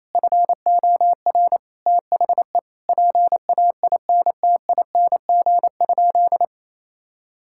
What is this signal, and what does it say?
Morse code, heard as a keyed tone, about 28 wpm, 700 Hz. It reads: FOR THE PAINTING?
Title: Sentences from Top 1000 Words - Review Part 6 - 28wpm